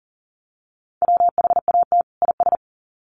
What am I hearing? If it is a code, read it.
Morse code, heard as a keyed tone, about 40 wpm, 700 Hz. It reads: WHAT IS
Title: Sets of 2 Words 40wpm